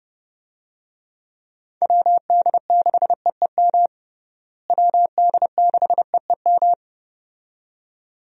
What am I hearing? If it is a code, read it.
Morse code, heard as a keyed tone, about 30 wpm, 700 Hz. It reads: WD6EEM WD6EEM
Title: USA Call Signs - 6 Characters - Repeated 2x - 30wpm